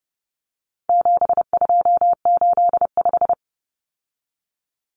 Morse code, heard as a keyed tone, about 30 wpm, 700 Hz. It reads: 7285
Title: Common Cut Numbers - 4-Digit 30wpm